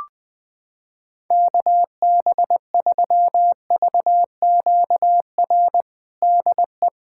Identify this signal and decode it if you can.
Morse code, heard as a keyed tone, about 20 wpm, 700 Hz. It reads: KB3VQR DE
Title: USA Callsign DE Callsign - 6 Characters - 20wpm